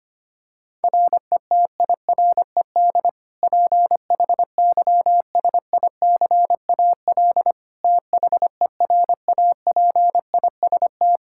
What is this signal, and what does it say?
Morse code, heard as a keyed tone, about 25 wpm, 700 Hz. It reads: RETIRED PHYSICAL THERAPIST
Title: Practice QSO - Exchange 3 - More Info - 25wpm